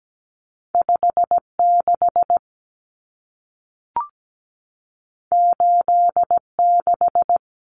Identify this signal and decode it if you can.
Morse code, heard as a keyed tone, about 17 wpm, 700 Hz. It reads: 56 86
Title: Numbers - 2-Digit 17wpm